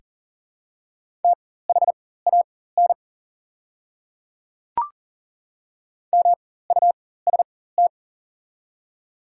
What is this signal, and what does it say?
Morse code, heard as a keyed tone, about 38 wpm, spaced as for 18 wpm, 690 Hz. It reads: THAN MUST
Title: 4x Character Spacing - Top 200 Words - 40wpm